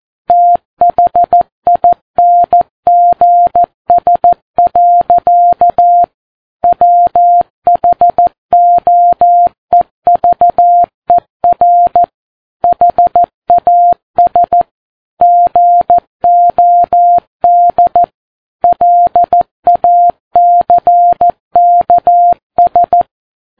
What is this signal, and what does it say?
Morse code, heard as a keyed tone, about 14 wpm, 700 Hz. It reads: THINGS. WHOEVER HAS GOD LACKS
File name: Ham Radio Sat